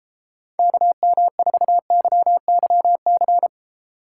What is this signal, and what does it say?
Morse code, heard as a keyed tone, about 32 wpm, 700 Hz. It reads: KM4YYC